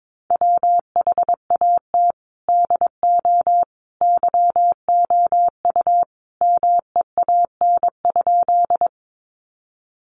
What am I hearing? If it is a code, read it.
Morse code, heard as a keyed tone, about 22 wpm, 700 Hz. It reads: WHAT DO YOU MEAN?
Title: Sentences from Top 200 Words 22wpm